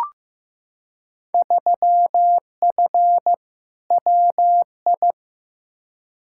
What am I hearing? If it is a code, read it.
Morse code, heard as a keyed tone, about 15 wpm, 700 Hz. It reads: 3F WI
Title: Field Day Exchanges - 15wpm